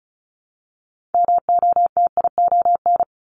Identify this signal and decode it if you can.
Morse code, heard as a keyed tone, about 36 wpm, 700 Hz. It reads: MOTION
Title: Multi-letter Phonemes Repeated 3x 35wpm